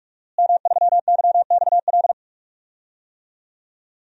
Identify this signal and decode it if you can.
Morse code, heard as a keyed tone, about 44 wpm, 700 Hz. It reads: M3YXL
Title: DXCC-100 Call Signs - 5 Characters - Repeated 2x - 45wpm